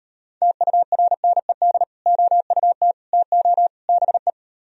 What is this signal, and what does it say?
Morse code, heard as a keyed tone, about 30 wpm, 700 Hz. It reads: TURNED OUT TO BE
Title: Speed-Racing - Sets of 4 Words 25wpm